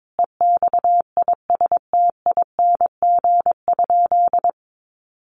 Morse code, heard as a keyed tone, about 22 wpm, 700 Hz. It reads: EXISTING?
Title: Sentences from Top 1500 Words 22wpm